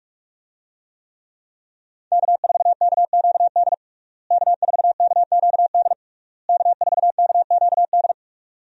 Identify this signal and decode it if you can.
Morse code, heard as a keyed tone, about 46 wpm, 700 Hz. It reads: K4KQD K4KQD K4KQD
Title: USA Call Signs - 5 Characters - Repeated 3x - 45wpm